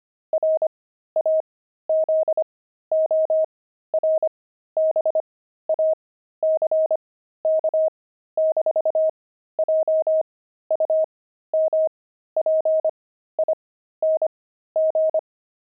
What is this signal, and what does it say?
Morse code, heard as a keyed tone, about 25 wpm, spaced as for 13 wpm, 630 Hz. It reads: RAZORBACK-JUMPING
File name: pangram-fs-13-cs-25-frogs